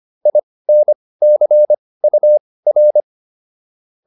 Morse code, heard as a keyed tone, about 25 wpm, spaced as for 18 wpm, 600 Hz. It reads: INCUR